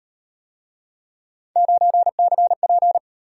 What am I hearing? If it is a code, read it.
Morse code, heard as a keyed tone, about 38 wpm, 700 Hz. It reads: 9CP